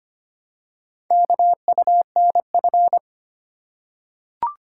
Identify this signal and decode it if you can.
Morse code, heard as a keyed tone, about 25 wpm, 700 Hz. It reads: KUNF
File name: practice / 25wpm / Four Letter-Number Combinations - 25wpm